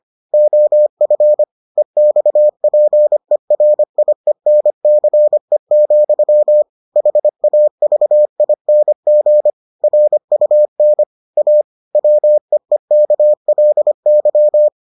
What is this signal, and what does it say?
Morse code, heard as a keyed tone, about 25 wpm, 600 Hz. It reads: OF EXPERIENCE, HAVING RUN A WEEKLY